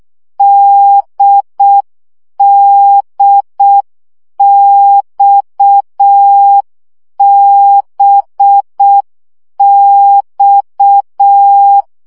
Morse code, heard as a keyed tone, about 6 wpm, 800 Hz. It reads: DDXBX
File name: CW gyakorlas d b x 5 baud